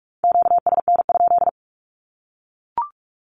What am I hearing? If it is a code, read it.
Morse code, heard as a keyed tone, about 44 wpm, 700 Hz. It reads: QSN?
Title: Q-Codes Repeated 3x 45wpm